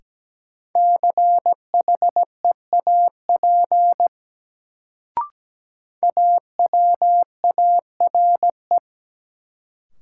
Morse code, heard as a keyed tone, about 17 wpm, 700 Hz. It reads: CHEAP AWARE
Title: Sentences from Top 1200 Words - New Word Drill Alt - 17wpm